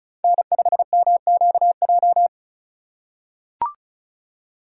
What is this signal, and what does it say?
Morse code, heard as a keyed tone, about 36 wpm, 700 Hz. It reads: N5MQJ
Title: USA Call Signs - 5 Characters - 4x Character Spacing - 35wpm